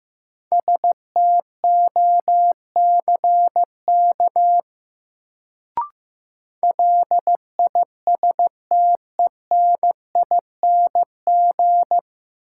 Morse code, heard as a keyed tone, about 16 wpm, 700 Hz. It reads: STOCK LISTENING
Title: Sets of 5 Words - New Word Drill - 15wpm